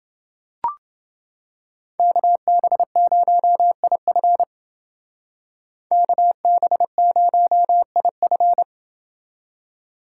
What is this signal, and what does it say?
Morse code, heard as a keyed tone, about 29 wpm, 700 Hz. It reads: KB0IF KB0IF